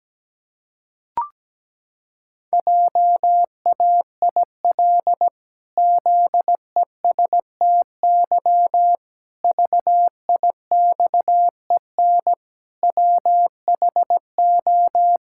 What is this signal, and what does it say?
Morse code, heard as a keyed tone, about 17 wpm, 700 Hz. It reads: JAIL ZESTY VIXEN WHO
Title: Pangram Challenge v2 - 17wpm